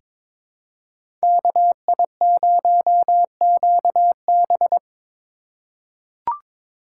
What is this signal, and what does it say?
Morse code, heard as a keyed tone, about 22 wpm, 700 Hz. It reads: KI0QB
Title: USA Call Signs - 3x Character Spacing - 22wpm